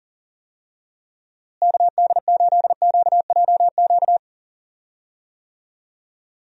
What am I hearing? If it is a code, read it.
Morse code, heard as a keyed tone, about 38 wpm, 700 Hz. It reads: KD8QJQ
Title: USA Call Signs - 6 Characters - Repeated 2x - 40wpm